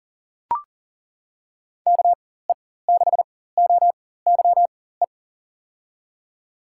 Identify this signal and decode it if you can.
Morse code, heard as a keyed tone, about 40 wpm, spaced as for 19 wpm, 700 Hz. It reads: KE6OYE
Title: USA Call Signs - 6 Characters - 4x Character Spacing - 40wpm